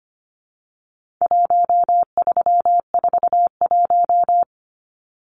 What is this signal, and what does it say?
Morse code, heard as a keyed tone, about 25 wpm, 700 Hz. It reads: 1341
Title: Numbers - 4-Digit 25wpm